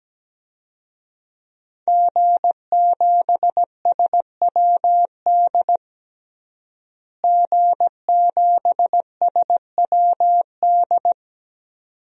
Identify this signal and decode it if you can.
Morse code, heard as a keyed tone, about 17 wpm, 700 Hz. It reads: G7SWD G7SWD
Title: DXCC-100 Call Signs - 5 Characters - Repeated 2x - 17wpm